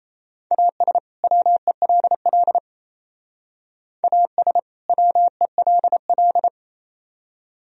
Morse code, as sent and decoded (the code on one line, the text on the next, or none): .- ... / .-- . .-.. .-.. / .- ... / .-- . .-.. .-..
AS WELL AS WELL